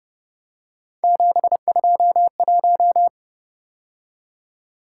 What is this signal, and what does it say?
Morse code, heard as a keyed tone, about 29 wpm, 700 Hz. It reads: 721
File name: Speed-Racing - Three Letter-Number Combinations - 25wpm